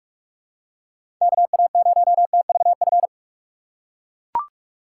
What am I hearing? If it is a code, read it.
Morse code, heard as a keyed tone, about 43 wpm, 700 Hz. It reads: KA0TVF
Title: USA Call Signs - 6 Characters - 4x Character Spacing - 45wpm